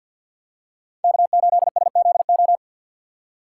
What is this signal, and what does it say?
Morse code, heard as a keyed tone, about 47 wpm, 700 Hz. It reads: K8SZO
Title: USA Call Signs - 5 Characters - 6x Character Spacing - 50wpm